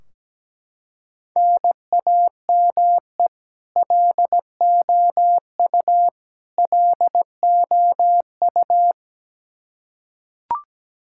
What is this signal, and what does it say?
Morse code, heard as a keyed tone, about 17 wpm, 700 Hz. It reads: NAME LOU LOU
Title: Practice QSO - Exchange 1 - Beginning - 17wpm